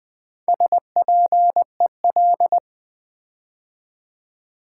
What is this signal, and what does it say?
Morse code, heard as a keyed tone, about 20 wpm, 700 Hz. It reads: SPEL
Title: Build A Word - 4 Letters At A Time - 8-Letter Words - 20wpm